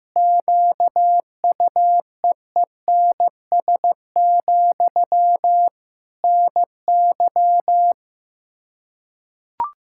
Morse code, heard as a keyed tone, about 15 wpm, 700 Hz. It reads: QUEENS, NY
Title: Practice QSO - Exchange 1 - Beginning - 15wpm